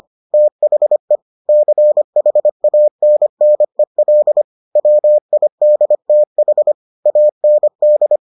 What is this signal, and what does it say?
Morse code, heard as a keyed tone, about 25 wpm, 600 Hz. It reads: THE CHANNEL WIDTH AND